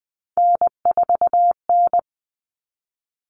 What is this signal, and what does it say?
Morse code, heard as a keyed tone, about 20 wpm, 700 Hz. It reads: N4N
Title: Common Cut Numbers - 3-Digit 20wpm